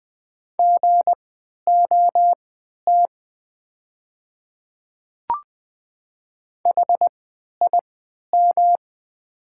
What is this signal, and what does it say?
Morse code, heard as a keyed tone, about 20 wpm, spaced as for 11 wpm, 700 Hz. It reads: GOT HIM
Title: 3x Character Spacing - Top 300 Words - 20wpm